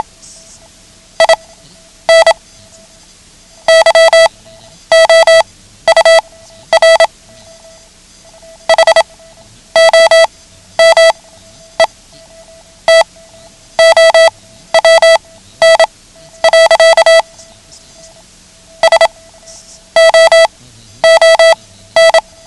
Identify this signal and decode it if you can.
Morse code, heard as a keyed tone, about 27 wpm, spaced as for 11 wpm, 660 Hz. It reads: IN YOUR HOME TOWN. SOON
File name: ham radio / CodeQuick / Code Quick 09